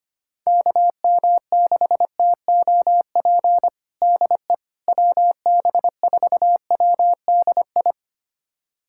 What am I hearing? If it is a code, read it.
Morse code, heard as a keyed tone, about 25 wpm, 700 Hz. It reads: KM6TOP DE WB4WDI
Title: USA Callsign DE Callsign - 6 Characters - 25wpm